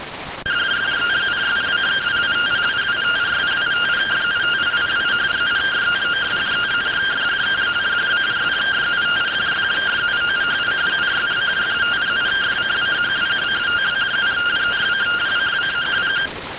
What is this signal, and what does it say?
Two-tone teleprinter signal, about 170 Hz shift.